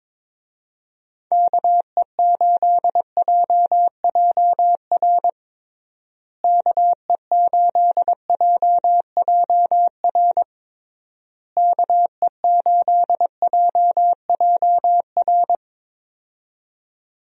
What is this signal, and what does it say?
Morse code, heard as a keyed tone, about 22 wpm, 700 Hz. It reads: KE8JJR KE8JJR KE8JJR
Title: USA Call Signs - 6 Characters - Repeated 3x - 22wpm